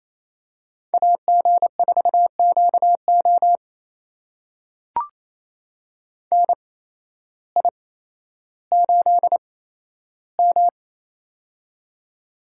Morse code, heard as a keyed tone, about 28 wpm, spaced as for 13 wpm, 700 Hz. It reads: AG4QO NI8M
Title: USA Call Signs - 8x Character Spacing - 28wpm